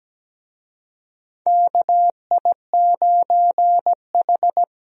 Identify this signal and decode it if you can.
Morse code, heard as a keyed tone, about 17 wpm, 700 Hz. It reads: KI9H
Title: USA Call Signs - 4 Characters - Repeated 2x - 17wpm